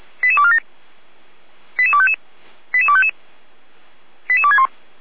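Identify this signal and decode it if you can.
Five-tone selective call (ZVEI), heard as quick runs of five tones, tones of about 70 ms. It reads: 88247 88255 88255 88261
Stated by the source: ZVEI five-tone selective call 88231 received by SDR.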